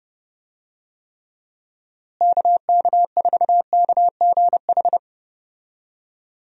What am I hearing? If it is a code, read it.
Morse code, heard as a keyed tone, about 32 wpm, 700 Hz. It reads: KK4KGH
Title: USA Call Signs - 6 Characters - Repeated 2x - 30wpm